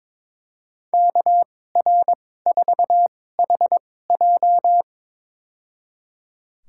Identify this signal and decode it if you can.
Morse code, heard as a keyed tone, about 22 wpm, spaced as for 16 wpm, 700 Hz. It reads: KR4HJ
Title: USA Call Signs - 5 Characters - 2x Character Spacing - 22wpm